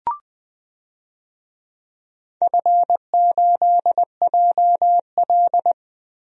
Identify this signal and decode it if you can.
Morse code, heard as a keyed tone, about 20 wpm, 700 Hz. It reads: F8JL
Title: DXCC-100 Call Signs - 4 Characters - 20wpm